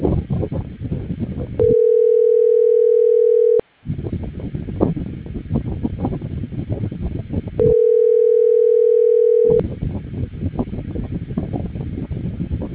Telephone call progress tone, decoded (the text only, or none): ringback tone